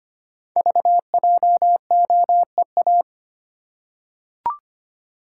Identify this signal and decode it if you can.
Morse code, heard as a keyed tone, about 25 wpm, 690 Hz. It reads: VJOEA
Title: Code Groups - 5 Letters - 25wpm